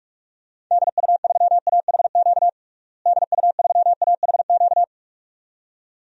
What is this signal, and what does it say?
Morse code, heard as a keyed tone, about 41 wpm, 700 Hz. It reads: DU3AHQ DU3AHQ
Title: DXCC-100 Call Signs - 6 Characters - Repeated 2x - 45wpm